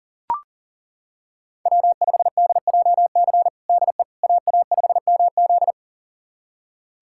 Morse code, heard as a keyed tone, about 37 wpm, 700 Hz. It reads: W5DJC DE AA5MZ
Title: USA Callsign DE Callsign - 5 Characters - 40wpm